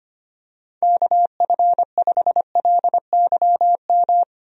Morse code, heard as a keyed tone, about 25 wpm, 700 Hz. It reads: KF5LYM
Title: USA Call Signs - 6 Characters - 25wpm